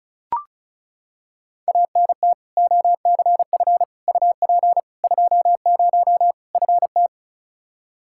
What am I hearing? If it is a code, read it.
Morse code, heard as a keyed tone, about 36 wpm, 700 Hz. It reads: ANT OCF UP 20 FT
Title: Practice QSO - Exchange 2 - Info - 35wpm